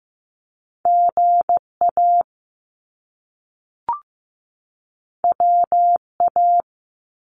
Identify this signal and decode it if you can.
Morse code, heard as a keyed tone, about 15 wpm, 700 Hz. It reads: GA WA
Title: US States 15wpm